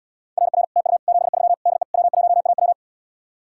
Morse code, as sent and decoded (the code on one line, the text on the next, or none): .-. .-. / - ..- / --... ...-- / -.. / .-- .---- .- .--
RR TU 73 DE W1AW